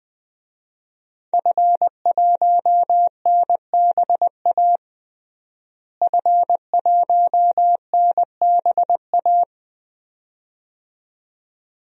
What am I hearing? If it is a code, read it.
Morse code, heard as a keyed tone, about 20 wpm, 700 Hz. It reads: F1NBA F1NBA
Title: DXCC-100 Call Signs - 5 Characters - Repeated 2x - 20wpm